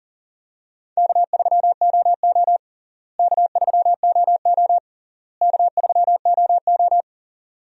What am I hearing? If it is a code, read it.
Morse code, heard as a keyed tone, about 40 wpm, 700 Hz. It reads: K3OO K3OO K3OO
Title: USA Call Signs - 4 Characters - Repeated 3x - 40wpm